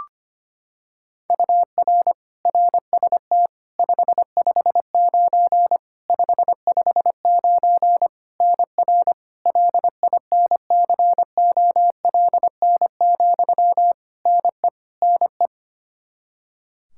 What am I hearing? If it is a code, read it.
Morse code, heard as a keyed tone, about 25 wpm, 700 Hz. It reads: UR RST 559 559 NR LINCOLN, NE NE
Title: Practice QSO - Exchange 1 - Beginning - 25wpm